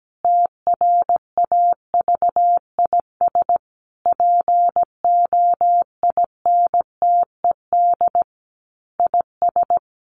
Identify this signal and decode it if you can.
Morse code, heard as a keyed tone, about 17 wpm, 700 Hz. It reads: TRAVIS POINTED IS
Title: A Sound Of Thunder 17wpm